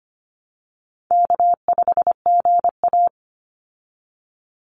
Morse code, heard as a keyed tone, about 25 wpm, 700 Hz. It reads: K5GA
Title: Easy Call Signs Repeated 3x - 25wpm